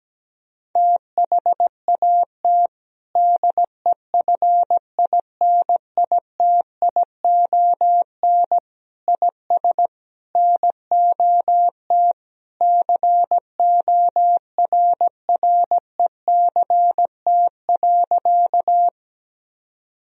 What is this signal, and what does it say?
Morse code, heard as a keyed tone, about 17 wpm, 700 Hz. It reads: THAT DEFINITION IS NOT CORRECT.
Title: Sentences from Top 1000 Words - Review Part 4 - 17wpm